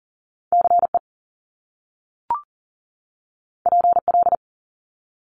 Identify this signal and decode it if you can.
Morse code, heard as a keyed tone, about 38 wpm, 700 Hz. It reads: CE PL
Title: Top Two Letter Combinations in Words 40wpm